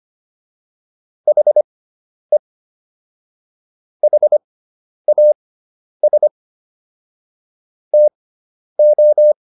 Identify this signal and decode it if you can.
Morse code, heard as a keyed tone, about 25 wpm, spaced as for 10 wpm, 600 Hz. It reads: HE HAS TO